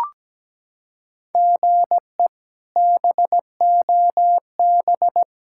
Morse code, heard as a keyed tone, about 17 wpm, 700 Hz. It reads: GE BOB
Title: K1USN SST QSO Practice - 17wpm